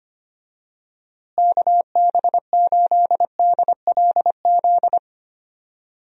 Morse code, heard as a keyed tone, about 25 wpm, 700 Hz. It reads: KB8DLZ
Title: USA Call Signs - 6 Characters - Repeated 3x - 25wpm